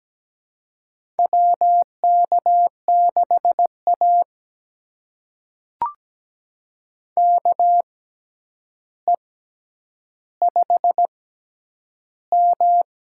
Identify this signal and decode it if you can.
Morse code, heard as a keyed tone, about 17 wpm, spaced as for 9 wpm, 700 Hz. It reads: WK6A KE5M
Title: USA Call Signs - 6x Character Spacing - 17wpm